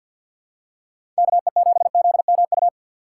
Morse code, heard as a keyed tone, about 53 wpm, 700 Hz. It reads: KE7ZMU
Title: USA Call Signs - 6 Characters - 3x Character Spacing - 50wpm